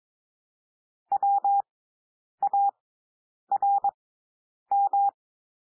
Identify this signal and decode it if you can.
Morse code, heard as a keyed tone, about 22 wpm, spaced as for 9 wpm, 800 Hz. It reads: WARM
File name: SOUND WORDS 22 wpm